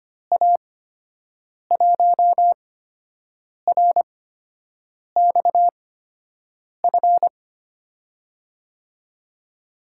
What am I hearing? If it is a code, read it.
Morse code, heard as a keyed tone, about 25 wpm, spaced as for 7 wpm, 700 Hz. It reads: A1RXF
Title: USA Call Signs - 6 Characters - 8x Character Spacing - 25wpm